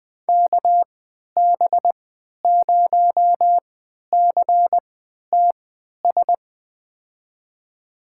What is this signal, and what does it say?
Morse code, heard as a keyed tone, about 21 wpm, spaced as for 11 wpm, 700 Hz. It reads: KB0CTS
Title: USA Call Signs - 6 Characters - 3x Character Spacing - 20wpm